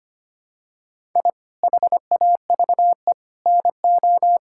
Morse code, heard as a keyed tone, about 25 wpm, 700 Hz. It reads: I HAVE NO
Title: Sentences from Top 1000 Words - Review Part 6 - 25wpm